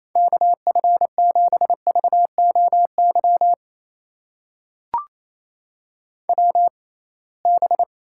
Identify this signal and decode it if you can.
Morse code, heard as a keyed tone, about 28 wpm, spaced as for 21 wpm, 700 Hz. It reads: KF7VOY WB
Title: USA Call Signs - 6 Characters - 6x Character Spacing - 28wpm